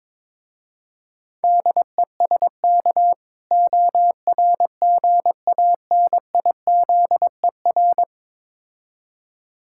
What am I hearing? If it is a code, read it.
Morse code, heard as a keyed tone, about 22 wpm, 690 Hz. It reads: DESK ORGANIZER
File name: Difficult Words with Hints - 22wpm